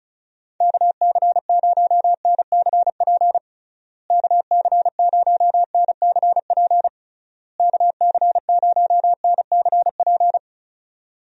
Morse code, heard as a keyed tone, about 34 wpm, 700 Hz. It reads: KC0NCP KC0NCP KC0NCP
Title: USA Call Signs - 6 Characters - Repeated 3x - 35wpm